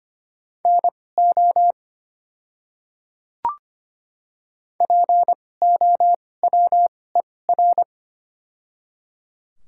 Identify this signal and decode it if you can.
Morse code, heard as a keyed tone, about 25 wpm, spaced as for 18 wpm, 690 Hz. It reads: NO POWER
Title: 2x Character Spacing - Top 500 Words - 25wpm